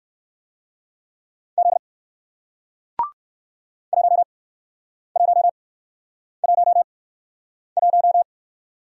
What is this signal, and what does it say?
Morse code, heard as a keyed tone, about 54 wpm, spaced as for 9 wpm, 700 Hz. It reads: Z 1111
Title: Speed-Racing Single Letter-Number 45wpm